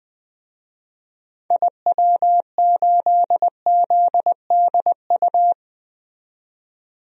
Morse code, heard as a keyed tone, about 20 wpm, 700 Hz. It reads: IW8ZDU
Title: DXCC-100 Call Signs - 6 Characters - 20wpm